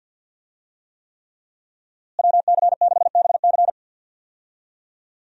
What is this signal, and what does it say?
Morse code, heard as a keyed tone, about 51 wpm, 700 Hz. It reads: WC6BC
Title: USA Call Signs - 5 Characters - Repeated 3x - 50wpm